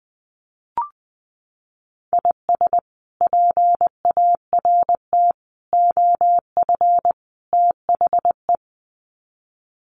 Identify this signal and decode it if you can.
Morse code, heard as a keyed tone, about 20 wpm, 700 Hz. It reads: IS PART OF THE
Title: Sets of 4 Words 20wpm